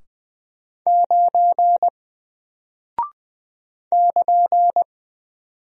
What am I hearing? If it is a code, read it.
Morse code, heard as a keyed tone, about 20 wpm, 700 Hz. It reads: 9 <KN>
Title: Single Character (Letter:Number:Punctuation:Prosign) Alt 20wpm